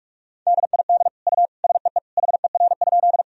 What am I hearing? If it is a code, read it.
Morse code, heard as a keyed tone, about 41 wpm, 700 Hz. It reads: DID U SEE HER?
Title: Sentences from Top 100 Words and Abbreviations 45wpm